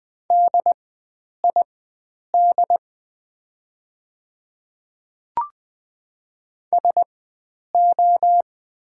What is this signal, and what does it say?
Morse code, heard as a keyed tone, about 20 wpm, spaced as for 9 wpm, 690 Hz. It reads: DID SO
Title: 4x Character Spacing - Top 100 Words - 20wpm